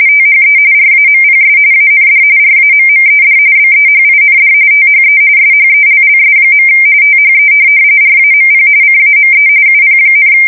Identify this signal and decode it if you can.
RTTY, about 45.45 baud. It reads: THE RAMPS AND RAMMED BEHIND THE SHELLS. THE RAMP CAME UP,
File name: fsk170 / USS-New-Jersey